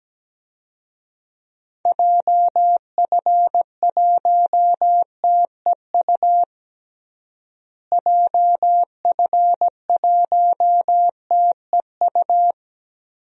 Morse code, heard as a keyed tone, about 17 wpm, 700 Hz. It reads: JF1TEU JF1TEU
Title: DXCC-100 Call Signs - 6 Characters - Repeated 2x - 17wpm